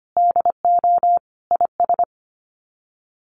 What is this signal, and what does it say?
Morse code, heard as a keyed tone, about 25 wpm, 700 Hz. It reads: DO IS
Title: Sets of 5 Words Encore2 25wpm